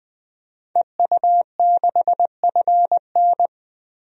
Morse code, heard as a keyed tone, about 20 wpm, 700 Hz. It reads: EU6FN
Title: DXCC-100 Call Signs - 5 Characters - 20wpm